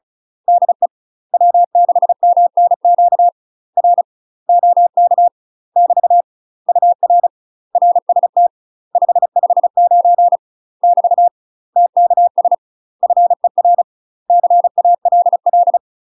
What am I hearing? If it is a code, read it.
Morse code, heard as a keyed tone, about 35 wpm, 700 Hz. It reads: DE W6MNQ R OK <BT> UR RST 559 <BT> TKS FER CALL
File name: UK MASTER QSOs TRAINING / 35WPM / UK MASTER QSOs 0012